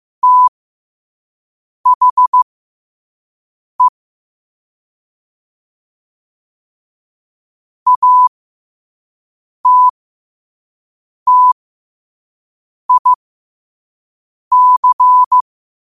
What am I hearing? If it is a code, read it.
Morse code, heard as a keyed tone, about 15 wpm, spaced as for 5 wpm, 1000 Hz. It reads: THE ATTIC
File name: qotdfiles / 5wpm